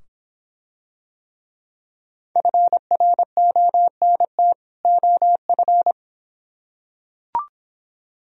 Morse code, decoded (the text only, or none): FRONT OF